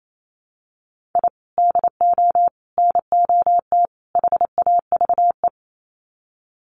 Morse code, heard as a keyed tone, about 28 wpm, 700 Hz. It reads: I DO NOT HAVE
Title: Sets of 4 Words 28wpm